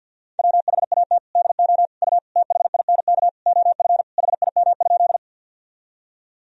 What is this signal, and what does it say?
Morse code, heard as a keyed tone, about 50 wpm, 700 Hz. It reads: WHAT DO U THINK OF HIM?
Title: Sentences from Top 200 Words and Abbreviations 50wpm